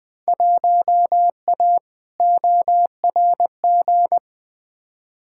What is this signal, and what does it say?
Morse code, heard as a keyed tone, about 20 wpm, 700 Hz. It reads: 1A ORG
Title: Field Day Exchanges - 20wpm